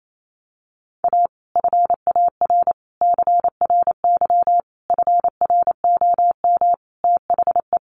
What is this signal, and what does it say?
Morse code, heard as a keyed tone, about 28 wpm, 700 Hz. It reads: A FAR CRY FROM THE
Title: Sets of 5 Words Encore 28wpm